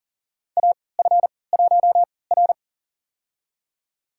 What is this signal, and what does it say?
Morse code, heard as a keyed tone, about 40 wpm, spaced as for 23 wpm, 700 Hz. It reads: AF1R
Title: USA Call Signs - 4 Characters - 3x Character Spacing - 40wpm